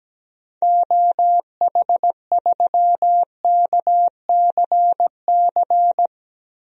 Morse code, heard as a keyed tone, about 17 wpm, 700 Hz. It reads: OH3KCC
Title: DXCC-100 Call Signs - 6 Characters - Repeated 3x - 17wpm